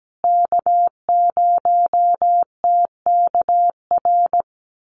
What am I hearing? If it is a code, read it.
Morse code, heard as a keyed tone, about 17 wpm, 700 Hz. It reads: K0TKR
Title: USA Call Signs Repeated 2x 17wpm